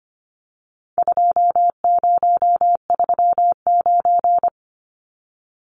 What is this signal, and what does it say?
Morse code, heard as a keyed tone, about 25 wpm, 700 Hz. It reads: 2039
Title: Numbers - 4-Digit 25wpm